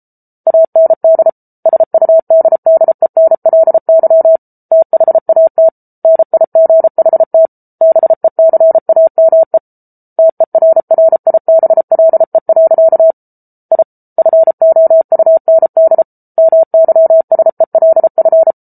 Morse code, heard as a keyed tone, about 34 wpm, 650 Hz. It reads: AND SUDDENLY THAT NIGHT BECAME TERRIBLE. I FOUND MYSELF